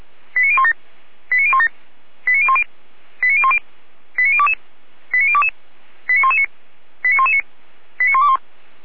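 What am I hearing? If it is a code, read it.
Five-tone selective call (ZVEI), heard as quick runs of five tones, tones of about 80 ms. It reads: 89017 89017 89010 89010 89022 89022 89119 89119 89121
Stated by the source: ZVEI five-tone selective call received on an SDR.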